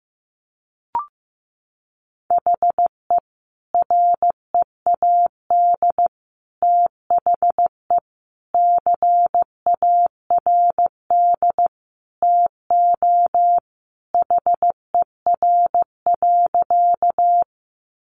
Morse code, heard as a keyed tone, about 15 wpm, 700 Hz. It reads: HE READ THE CARD TO HER.
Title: Sentences from Top 600 Words 15wpm